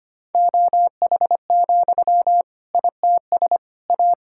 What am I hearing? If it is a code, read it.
Morse code, heard as a keyed tone, about 25 wpm, 700 Hz. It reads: OH, ITS A
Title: Sentences from Top 1000 Words - Review Part 5 - 25wpm